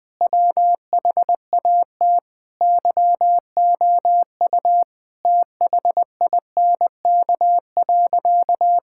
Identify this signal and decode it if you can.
Morse code, heard as a keyed tone, about 20 wpm, 700 Hz. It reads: WHAT YOU THINK.
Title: Sentences with 6-Word Phrases - Spoken Beginning - 20wpm